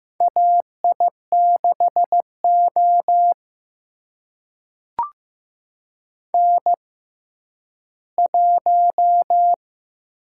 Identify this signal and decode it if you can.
Morse code, heard as a keyed tone, about 15 wpm, spaced as for 10 wpm, 700 Hz. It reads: AI6O N1
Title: USA Call Signs - 4 Characters - 6x Character Spacing - 15wpm